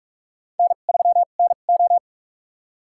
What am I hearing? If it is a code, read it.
Morse code, heard as a keyed tone, about 46 wpm, spaced as for 33 wpm, 700 Hz. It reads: N3NO
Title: USA Call Signs - 4 Characters - 2x Character Spacing - 45wpm